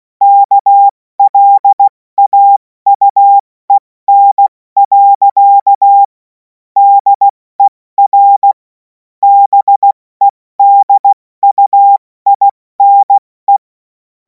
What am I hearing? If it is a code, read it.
Morse code, heard as a keyed tone, about 16 wpm, spaced as for 14 wpm, 800 Hz. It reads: KLAUEN. DER BEDUINE
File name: Morsekurs / Witze / Witz 03. März 2026 70 F80